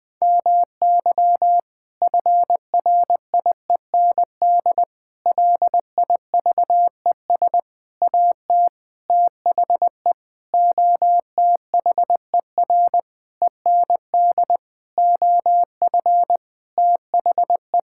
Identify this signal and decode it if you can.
Morse code, heard as a keyed tone, about 20 wpm, 700 Hz. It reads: MY FRIEND LIVES AT THE OTHER END OF THE
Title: Sentences with 5-Word Phrases - Spoken Beginning - 20wpm